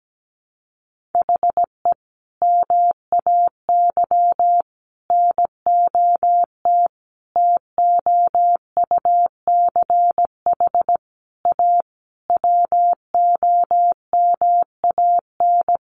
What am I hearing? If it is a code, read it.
Morse code, heard as a keyed tone, about 17 wpm, 700 Hz. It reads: HE MAY NOT TOUCH A WOMAN
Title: Sentences from Top 1500 Words 17wpm